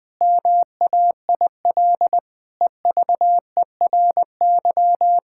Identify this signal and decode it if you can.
Morse code, heard as a keyed tone, about 20 wpm, 700 Hz. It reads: MAIL EVERY
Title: Difficult Sentences with Hints - 20wpm